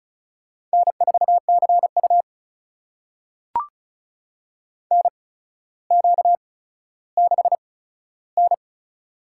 Morse code, heard as a keyed tone, about 36 wpm, spaced as for 15 wpm, 700 Hz. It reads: N4CU NQ6N
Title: USA Call Signs - 4 Characters - 8x Character Spacing - 35wpm